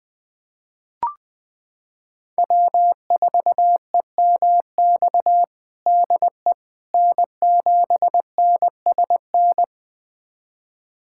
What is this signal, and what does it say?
Morse code, heard as a keyed tone, about 20 wpm, 700 Hz. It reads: W4EMX DE N7NSN
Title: USA Callsign DE Callsign - 5 Characters - 20wpm